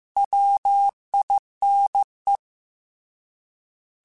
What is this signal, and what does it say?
Morse code, heard as a keyed tone, about 15 wpm, 800 Hz. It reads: WINE
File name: Alphabetical / Alphabetical mp3s / Ws @13